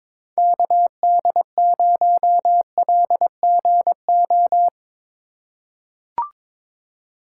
Morse code, heard as a keyed tone, about 22 wpm, 700 Hz. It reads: KD0LGO
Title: USA Call Signs - 6 Characters - 3x Character Spacing - 22wpm